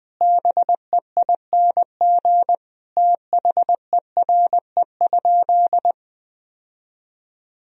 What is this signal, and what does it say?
Morse code, heard as a keyed tone, about 20 wpm, 700 Hz. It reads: BEING THERE?
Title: Sentences from Top 1000 Words - Review Part 3 - 20wpm